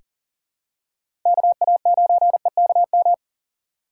Morse code, heard as a keyed tone, about 40 wpm, 700 Hz. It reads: KA9EKM
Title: USA Call Signs - 6 Characters - 2x Character Spacing - 40wpm